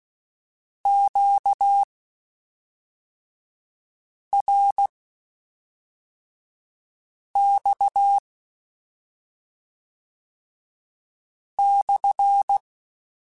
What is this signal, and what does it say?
Morse code, heard as a keyed tone, about 16 wpm, spaced as for 3 wpm, 800 Hz. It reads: QRX /